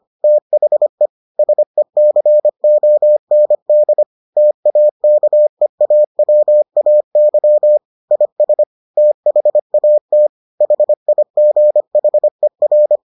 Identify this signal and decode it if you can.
Morse code, heard as a keyed tone, about 25 wpm, 600 Hz. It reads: THE SECOND TAKEAWAY IS THAT HIGHER